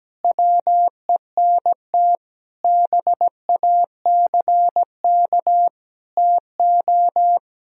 Morse code, heard as a keyed tone, about 17 wpm, 700 Hz. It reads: WENT BACK TO
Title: Sentences from Top 1000 Words - Review Part 1 - 17wpm